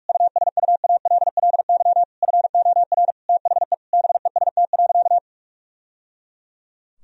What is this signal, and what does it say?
Morse code, heard as a keyed tone, about 46 wpm, 700 Hz. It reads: USUALLY FOR THE BEST.
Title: Sentences from Top 1000 Words - Review Part 3 - 45wpm